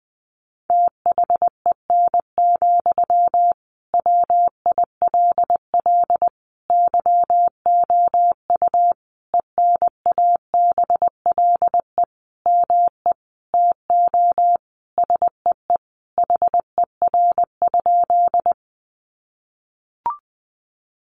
Morse code, heard as a keyed tone, about 20 wpm, 700 Hz. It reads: THEN, WILL YOU ENABLE ME TO SEE HER?
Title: Sentences from Top 1500 Words 20wpm